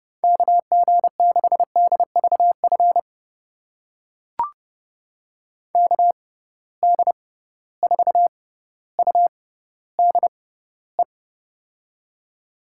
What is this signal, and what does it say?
Morse code, heard as a keyed tone, about 30 wpm, spaced as for 15 wpm, 700 Hz. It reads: KG6DVF KD4UDE
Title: USA Call Signs - 6 Characters - 6x Character Spacing - 30wpm